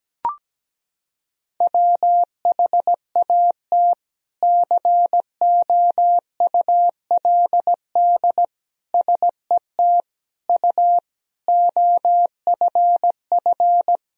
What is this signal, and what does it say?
Morse code, heard as a keyed tone, about 17 wpm, 700 Hz. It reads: WHAT COULD SET U OFF
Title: Sentences from Top 200 Words and Abbreviations 17wpm